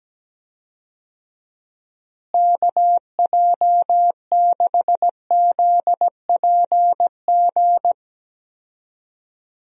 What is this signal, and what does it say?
Morse code, heard as a keyed tone, about 17 wpm, 700 Hz. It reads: KJ6ZPG
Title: USA Call Signs - 6 Characters - Repeated 2x - 17wpm